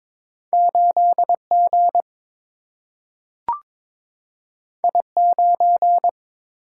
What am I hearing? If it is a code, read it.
Morse code, heard as a keyed tone, about 22 wpm, 700 Hz. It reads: 8 G I 9
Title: Single-Letter Single-Number 22wpm